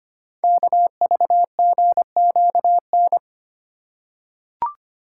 Morse code, heard as a keyed tone, about 25 wpm, 700 Hz. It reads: KVGQN
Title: Code Groups - 5 Letters - 25wpm